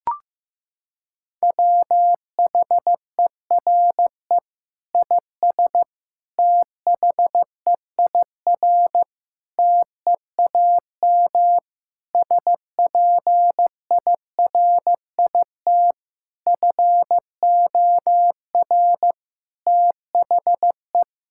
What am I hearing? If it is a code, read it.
Morse code, heard as a keyed tone, about 15 wpm, 700 Hz. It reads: WHERE IS THEIR TEAM SPIRIT FOR THE
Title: Sentences from Top 1600 Words 15wpm